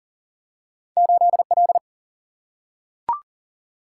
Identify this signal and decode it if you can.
Morse code, heard as a keyed tone, about 40 wpm, 700 Hz. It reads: 8L
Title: Single-Letter Single-Number 40wpm